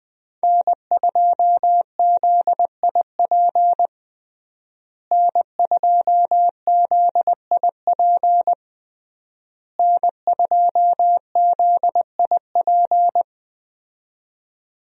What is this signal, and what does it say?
Morse code, heard as a keyed tone, about 20 wpm, 700 Hz. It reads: N2ZIP N2ZIP N2ZIP
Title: USA Call Signs - 5 Characters - Repeated 3x - 20wpm